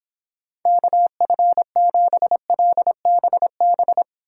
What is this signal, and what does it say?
Morse code, heard as a keyed tone, about 29 wpm, 700 Hz. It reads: KF7LBB